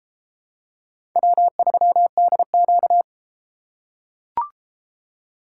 Morse code, heard as a keyed tone, about 32 wpm, 700 Hz. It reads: W3DQ